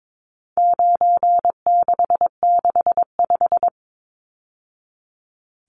Morse code, heard as a keyed tone, about 22 wpm, 700 Hz. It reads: 9665
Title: Numbers - 4-Digit 22wpm